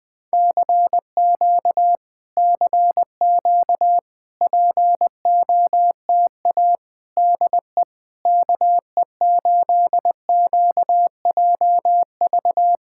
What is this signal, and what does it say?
Morse code, heard as a keyed tone, about 20 wpm, 700 Hz. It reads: CQ CQ POTA DE KE8QJV
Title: POTA QSO Practice - 20wpm